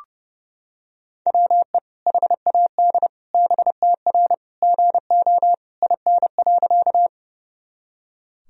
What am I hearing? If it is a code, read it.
Morse code, heard as a keyed tone, about 32 wpm, 700 Hz. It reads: WE HAD BTR GO IN.
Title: Sentences from Top 400 Words and Abbreviations 30wpm